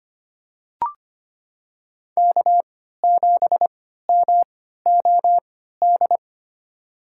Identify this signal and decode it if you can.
Morse code, heard as a keyed tone, about 25 wpm, spaced as for 14 wpm, 700 Hz. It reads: K7MOD
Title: USA Call Signs - 3x Character Spacing - 25wpm